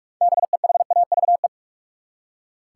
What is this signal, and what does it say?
Morse code, heard as a keyed tone, about 47 wpm, 700 Hz. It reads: BEHAVE
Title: Six Letter Words - 45wpm